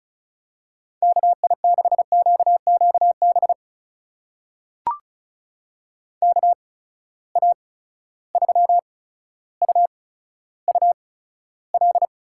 Morse code, heard as a keyed tone, about 36 wpm, spaced as for 15 wpm, 700 Hz. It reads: KI6QQB KA3UUL
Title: USA Call Signs - 8x Character Spacing - 35wpm